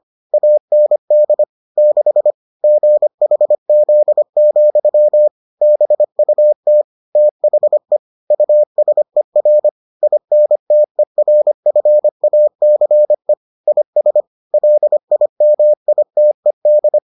Morse code, heard as a keyed tone, about 25 wpm, 600 Hz. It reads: AND 6 GHZ, BUT THE USER INTERFACE IS LIMITED